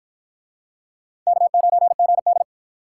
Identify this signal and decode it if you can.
Morse code, heard as a keyed tone, about 51 wpm, 700 Hz. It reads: K9GD